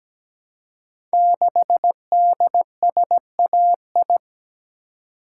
Morse code, heard as a keyed tone, about 17 wpm, 700 Hz. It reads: 6DSAI
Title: Code Groups - 5 Letter:Numbers - 17wpm